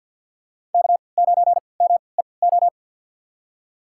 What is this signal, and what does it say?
Morse code, heard as a keyed tone, about 48 wpm, spaced as for 28 wpm, 700 Hz. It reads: K9MEO
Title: USA Call Signs - 5 Characters - 3x Character Spacing - 50wpm